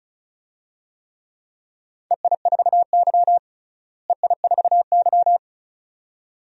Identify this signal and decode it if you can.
Morse code, heard as a keyed tone, about 36 wpm, 700 Hz. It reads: EI4Y EI4Y
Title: DXCC-100 Call Signs - 4 Characters - Repeated 2x - 35wpm